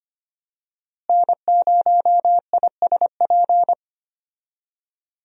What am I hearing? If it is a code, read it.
Morse code, heard as a keyed tone, about 25 wpm, 700 Hz. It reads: N0ISP
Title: USA Call Signs - 5 Characters - Repeated 3x - 25wpm